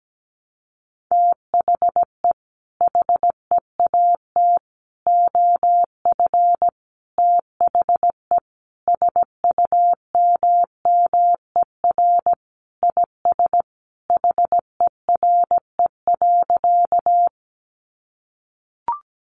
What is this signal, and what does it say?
Morse code, heard as a keyed tone, about 17 wpm, 700 Hz. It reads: THE HEAT OF THE SUMMER IS HERE.
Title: Sentences from Top 1100 Words 17wpm